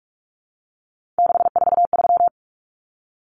Morse code, heard as a keyed tone, about 44 wpm, 700 Hz. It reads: 643
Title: Numbers - 3-Digit 45wpm